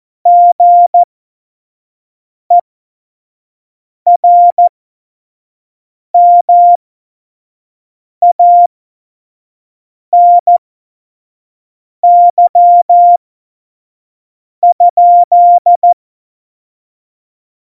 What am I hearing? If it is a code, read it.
Morse code, heard as a keyed tone, about 14 wpm, spaced as for 5 wpm, 700 Hz. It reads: GERMANY?
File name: Morsekurs / Witze / Witz 30. Juli 2019, 25 F70